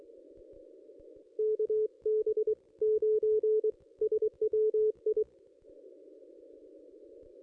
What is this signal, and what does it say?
Morse code, heard as a keyed tone, about 23 wpm, spaced as for 20 wpm, 430 Hz. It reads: KB9SWI
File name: contests / ss-cw-k5tr-op-n5rz -1400